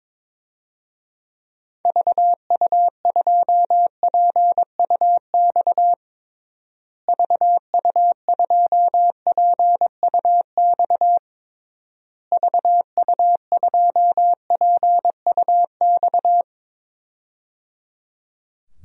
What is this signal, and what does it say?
Morse code, heard as a keyed tone, about 22 wpm, 700 Hz. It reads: VU2PUX VU2PUX VU2PUX
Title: DXCC-100 Call Signs - 6 Characters - Repeated 3x - 22wpm